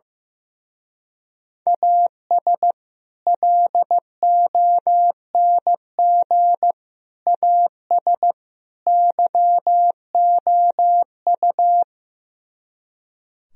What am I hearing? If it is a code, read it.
Morse code, heard as a keyed tone, about 15 wpm, 700 Hz. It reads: AS LONG AS YOU
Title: Speed-Racing - Sets of 4 Words 15wpm